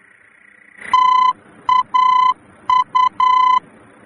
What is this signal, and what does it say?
Morse code, heard as a keyed tone, about 9 wpm, 1010 Hz. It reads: TAU
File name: germany / mx / TAU-D-VOR